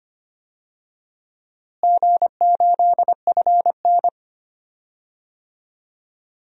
Morse code, heard as a keyed tone, about 25 wpm, 700 Hz. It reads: G8FN
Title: DXCC-100 Call Signs - 4 Characters - Repeated 3x - 25wpm